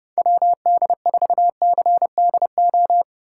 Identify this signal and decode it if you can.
Morse code, heard as a keyed tone, about 30 wpm, 700 Hz. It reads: WD4CDO